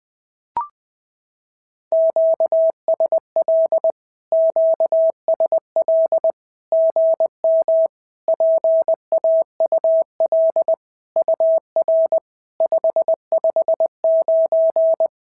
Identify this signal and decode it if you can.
Morse code, heard as a keyed tone, about 20 wpm, 650 Hz. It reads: QSL QSL GM PAUL UR 559
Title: SOTA QSO Practice 20wpm